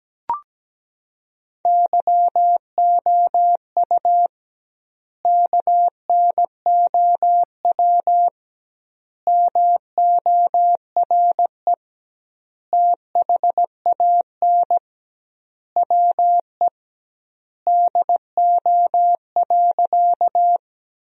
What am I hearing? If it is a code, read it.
Morse code, heard as a keyed tone, about 17 wpm, 700 Hz. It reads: YOU KNOW MORE THAN WE DO.
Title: 2x Word-Spacing - Sentences from Top 100 Words - 17wpm